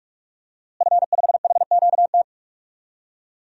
Morse code, heard as a keyed tone, about 42 wpm, 700 Hz. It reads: F5HQT
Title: DXCC-100 Call Signs - 5 Characters - Repeated 2x - 45wpm